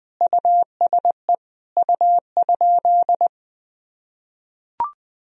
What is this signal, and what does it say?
Morse code, heard as a keyed tone, about 20 wpm, 700 Hz. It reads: USE U?
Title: Sentences from Top 200 Words and Abbreviations 20wpm